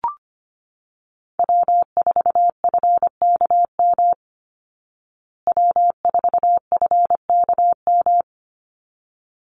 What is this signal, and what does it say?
Morse code, heard as a keyed tone, about 25 wpm, 700 Hz. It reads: W4FKM W4FKM
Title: Call Signs Repeated 3x - 25wpm